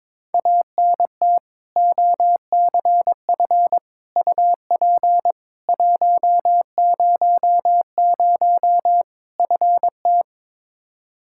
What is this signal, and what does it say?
Morse code, heard as a keyed tone, about 22 wpm, 700 Hz. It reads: ANT OCF UP 100 FT
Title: Practice QSO - Exchange 2 - Info - 22wpm